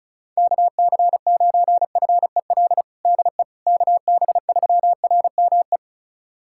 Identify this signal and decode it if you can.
Morse code, heard as a keyed tone, about 36 wpm, 700 Hz. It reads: KC9FEL DE KB3RME
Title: USA Callsign DE Callsign - 6 Characters - 35wpm